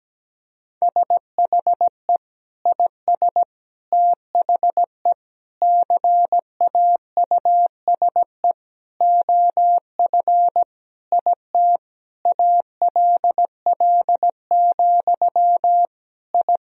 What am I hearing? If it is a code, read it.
Morse code, heard as a keyed tone, about 17 wpm, 700 Hz. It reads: SHE IS THE CAUSE OF IT ALL, I
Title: Sentences from Top 1000 Words - Review Part 1 - 17wpm